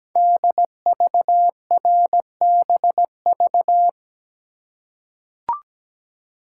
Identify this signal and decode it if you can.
Morse code, heard as a keyed tone, about 17 wpm, 700 Hz. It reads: DVRBV
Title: Code Groups - 5 Letters - 17wpm